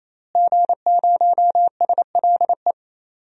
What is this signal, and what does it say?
Morse code, heard as a keyed tone, about 28 wpm, 700 Hz. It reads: G0SLE
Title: DXCC-100 Call Signs - 5 Characters - Repeated 3x - 28wpm